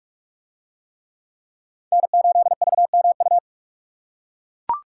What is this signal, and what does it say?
Morse code, heard as a keyed tone, about 41 wpm, 700 Hz. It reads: N8VMU